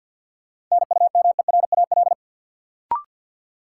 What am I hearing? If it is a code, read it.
Morse code, heard as a keyed tone, about 53 wpm, 700 Hz. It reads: NUMERAL
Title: Sentences from Top 1000 Words - Review Part 2 - New Word Drill - 50wpm